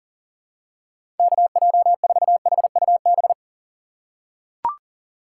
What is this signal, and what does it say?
Morse code, heard as a keyed tone, about 40 wpm, 700 Hz. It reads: KJ4HUB
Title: USA Call Signs - 6 Characters - 4x Character Spacing - 40wpm